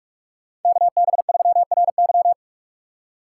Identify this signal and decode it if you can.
Morse code, heard as a keyed tone, about 46 wpm, 700 Hz. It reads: KB3RY